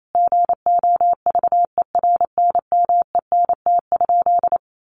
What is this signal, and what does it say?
Morse code, heard as a keyed tone, about 28 wpm, 700 Hz. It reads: GOVERNMENT?
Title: Sentences from Top 1500 Words 28wpm